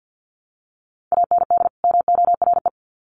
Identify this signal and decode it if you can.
Morse code, heard as a keyed tone, about 50 wpm, 700 Hz. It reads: AND MORE
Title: Sets of 2 Words 50wpm